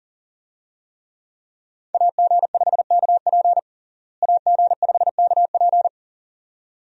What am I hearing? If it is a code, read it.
Morse code, heard as a keyed tone, about 40 wpm, 700 Hz. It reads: AG5KP AG5KP
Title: USA Call Signs - 5 Characters - Repeated 2x - 40wpm